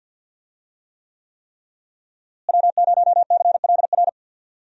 Morse code, heard as a keyed tone, about 49 wpm, 700 Hz. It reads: W0CLR
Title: USA Call Signs - 5 Characters - Repeated 3x - 50wpm